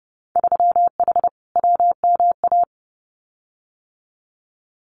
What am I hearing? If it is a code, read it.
Morse code, heard as a keyed tone, about 30 wpm, 700 Hz. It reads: 3H WMA
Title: Winter Field Day Exchanges Encore 30wpm